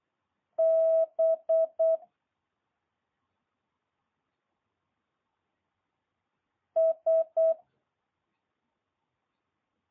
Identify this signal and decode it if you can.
Morse code, heard as a keyed tone, about 8 wpm, 650 Hz. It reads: B S
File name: jh8cbh